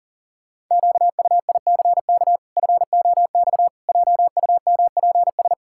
Morse code, heard as a keyed tone, about 40 wpm, 700 Hz. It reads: QUICK FOX JUMPS
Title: Pangram Challenge v2 - 40wpm